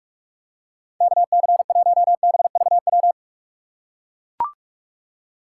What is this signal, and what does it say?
Morse code, heard as a keyed tone, about 44 wpm, 700 Hz. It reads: KC1BVW
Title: USA Call Signs - 6 Characters - 2x Character Spacing - 45wpm